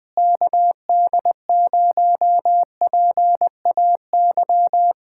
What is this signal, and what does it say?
Morse code, heard as a keyed tone, about 20 wpm, 700 Hz. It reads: KD0PAY
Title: Speed-Racing Common QSO Phrases Encore - 17wpm